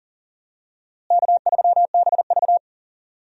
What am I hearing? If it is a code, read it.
Morse code, heard as a keyed tone, about 40 wpm, 700 Hz. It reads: K3BV
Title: USA Call Signs - 4 Characters - Repeated 3x - 40wpm